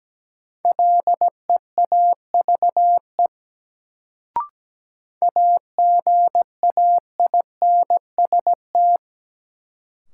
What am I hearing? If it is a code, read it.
Morse code, heard as a keyed tone, about 17 wpm, 700 Hz. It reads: LEAVE AGAINST
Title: Sentences from Top 400 Words - New Word Drill - 17wpm